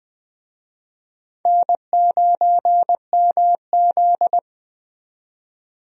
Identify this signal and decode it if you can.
Morse code, heard as a keyed tone, about 20 wpm, 700 Hz. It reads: N9MZ
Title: USA Call Signs - 4 Characters - 20wpm